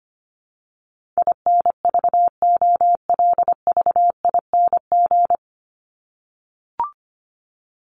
Morse code, heard as a keyed tone, about 25 wpm, 700 Hz. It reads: INVOLVING
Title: ING Suffix - Top 100 Words with Base Words 25wpm